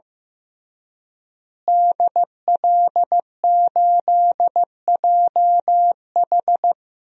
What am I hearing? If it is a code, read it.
Morse code, heard as a keyed tone, about 15 wpm, 700 Hz. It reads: DL8JH
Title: DXCC-100 Call Signs - 5 Characters - Repeated 3x - 15wpm